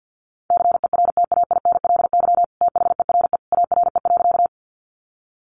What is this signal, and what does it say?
Morse code, heard as a keyed tone, about 49 wpm, 700 Hz. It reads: CERTAINLY THERE ARE.
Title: Sentences from Top 900 Words 50wpm